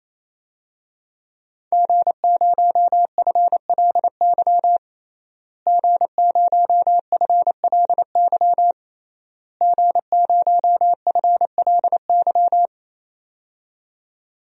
Morse code, heard as a keyed tone, about 28 wpm, 700 Hz. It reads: G0FLY G0FLY G0FLY
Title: DXCC-100 Call Signs - 5 Characters - Repeated 3x - 28wpm